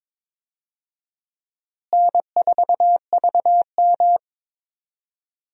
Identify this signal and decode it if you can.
Morse code, heard as a keyed tone, about 22 wpm, 700 Hz. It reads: N4VM
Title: USA Call Signs - 4 Characters - Repeated 3x - 22wpm